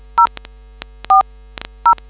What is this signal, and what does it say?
DTMF, heard as telephone touch tones, tones of about 100 ms. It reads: 040